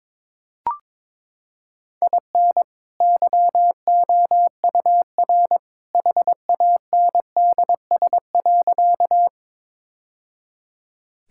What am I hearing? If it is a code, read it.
Morse code, heard as a keyed tone, about 22 wpm, 700 Hz. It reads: IN YOUR HANDS.
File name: practice / 22wpm / Sentences with 2-Word Phrases - 22wpm